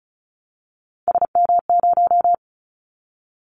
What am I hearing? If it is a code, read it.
Morse code, heard as a keyed tone, about 36 wpm, 700 Hz. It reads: SM0
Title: Callsign Prefixes for Top 25 DXCC - 35wpm